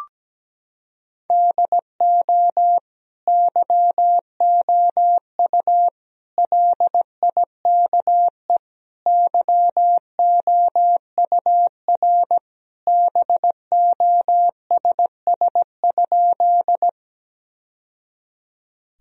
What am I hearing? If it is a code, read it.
Morse code, heard as a keyed tone, about 17 wpm, 700 Hz. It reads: DO YOU LIKE YOUR BOSS?
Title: Sentences from Top 1000 Words - Review Part 4 - 17wpm